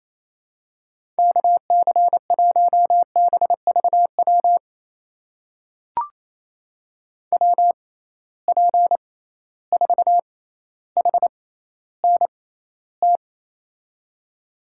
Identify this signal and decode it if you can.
Morse code, heard as a keyed tone, about 28 wpm, spaced as for 14 wpm, 700 Hz. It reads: KC1BVW WP4HNT
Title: USA Call Signs - 6 Characters - 6x Character Spacing - 28wpm